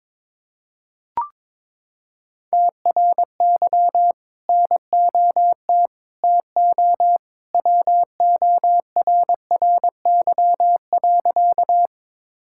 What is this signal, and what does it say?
Morse code, heard as a keyed tone, about 22 wpm, 700 Hz. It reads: TRY NOT TO WORRY.
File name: practice / 22wpm / Sentences with 2-Word Phrases - 22wpm